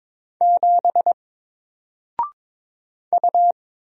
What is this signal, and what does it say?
Morse code, heard as a keyed tone, about 22 wpm, 700 Hz. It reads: 7 U
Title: Single Character (Letter:Number:Punctuation:Prosign) Alt 22wpm